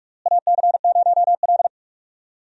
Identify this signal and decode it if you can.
Morse code, heard as a keyed tone, about 47 wpm, 700 Hz. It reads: AC0L
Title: USA Call Signs - 4 Characters - Repeated 3x - 45wpm